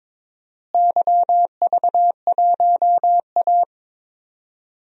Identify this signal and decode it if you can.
Morse code, heard as a keyed tone, about 22 wpm, 700 Hz. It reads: YV1A
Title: DXCC-100 Call Signs - 4 Characters - Repeated 2x - 22wpm